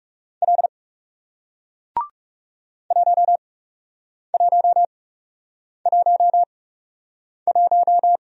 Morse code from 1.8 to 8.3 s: 1111

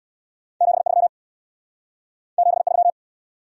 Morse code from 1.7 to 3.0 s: 73

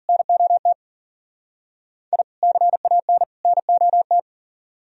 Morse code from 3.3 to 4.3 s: NOT

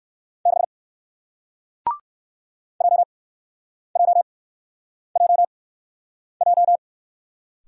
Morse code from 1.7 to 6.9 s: JJJJ

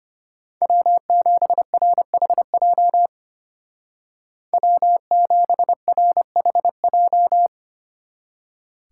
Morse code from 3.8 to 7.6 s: W7RHJ

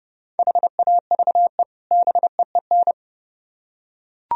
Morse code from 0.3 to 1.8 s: HAVE

